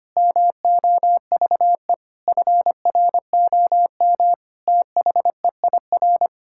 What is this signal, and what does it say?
Morse code, heard as a keyed tone, about 25 wpm, 700 Hz. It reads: MOVE FROM THEIR
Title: Sentences from Top 1000 Words - Review Part 6 - 25wpm